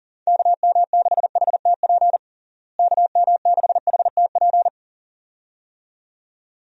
Morse code, heard as a keyed tone, about 40 wpm, 700 Hz. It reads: KM6HTP KM6HTP
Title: USA Call Signs - 6 Characters - Repeated 2x - 40wpm